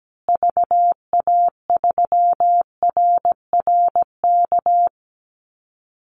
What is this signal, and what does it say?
Morse code, heard as a keyed tone, about 17 wpm, 700 Hz. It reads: VA3RRK
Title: Call Signs Repeated 3x - 17wpm